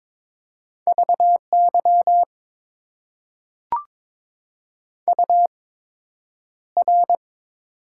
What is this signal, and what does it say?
Morse code, heard as a keyed tone, about 22 wpm, spaced as for 9 wpm, 700 Hz. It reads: VY UR
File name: Top 90 QSO Elements - 8x Character Spacing - 22wpm